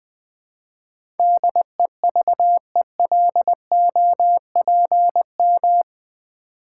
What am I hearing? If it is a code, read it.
Morse code, heard as a keyed tone, about 20 wpm, 700 Hz. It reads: DEVELOPM
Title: Build A Word - 4 Letters At A Time - 11-Letter Words - 20wpm